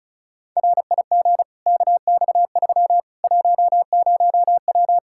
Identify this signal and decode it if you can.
Morse code, heard as a keyed tone, about 36 wpm, 700 Hz. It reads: RIG KX3 10W
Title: Practice QSO - Exchange 2 - Info - 35wpm